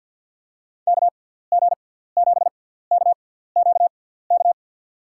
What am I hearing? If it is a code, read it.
Morse code, heard as a keyed tone, about 52 wpm, spaced as for 17 wpm, 700 Hz. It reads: KG7KQK
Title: USA Call Signs - 6x Character Spacing - 50wpm